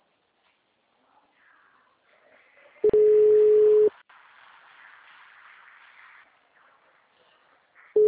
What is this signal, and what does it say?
Ringback tone (Europe, 425 Hz).
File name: out- -2017- - - .150